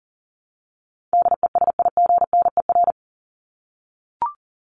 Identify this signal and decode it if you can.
Morse code, heard as a keyed tone, about 40 wpm, 700 Hz. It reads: DESIGNER
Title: Top 1501-1750 Words 40wpm